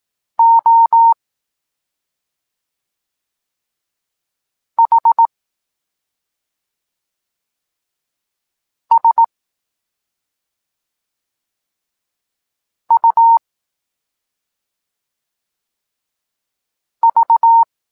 Morse code, heard as a keyed tone, about 18 wpm, spaced as for 2 wpm, 920 Hz. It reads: OHSUV
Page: Tekst 8 – doskonalenie odbioru poznanych liter